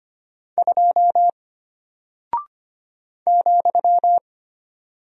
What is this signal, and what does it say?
Morse code, heard as a keyed tone, about 25 wpm, 700 Hz. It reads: 2 ,
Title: Single Character (Letter:Number:Punctuation:Prosign) Alt 25wpm